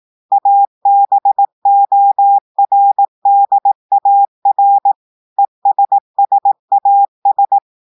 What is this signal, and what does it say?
Morse code, heard as a keyed tone, about 18 wpm, 800 Hz. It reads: ABORDAR ESSAS
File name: audios / portugais / portugais texte en clair 2 vitesse 18 mots minutes